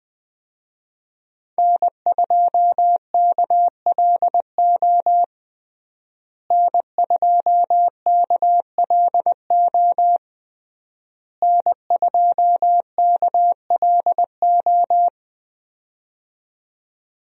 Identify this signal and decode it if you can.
Morse code, heard as a keyed tone, about 21 wpm, 700 Hz. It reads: N2KLO N2KLO N2KLO
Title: USA Call Signs - 5 Characters - Repeated 3x - 20wpm